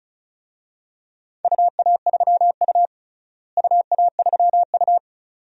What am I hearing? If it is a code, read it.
Morse code, heard as a keyed tone, about 34 wpm, 700 Hz. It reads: UA3U UA3U
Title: DXCC-100 Call Signs - 4 Characters - Repeated 2x - 35wpm